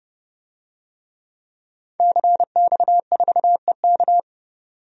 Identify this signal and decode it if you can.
Morse code, heard as a keyed tone, about 32 wpm, 700 Hz. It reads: CX4EK
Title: DXCC-100 Call Signs - 5 Characters - 30wpm